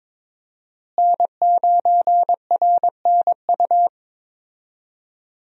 Morse code, heard as a keyed tone, about 22 wpm, 700 Hz. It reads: N9RNU
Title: USA Call Signs - 5 Characters - Repeated 2x - 22wpm